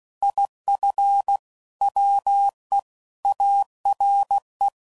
Morse code, heard as a keyed tone, about 16 wpm, 800 Hz. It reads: IF WE ARE
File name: CW Code Groups @15WPM / QSOs / WA7QOX